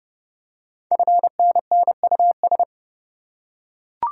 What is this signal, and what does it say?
Morse code, heard as a keyed tone, about 28 wpm, 690 Hz. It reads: FNNUS
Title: Code Groups - 5 Letters - 30wpm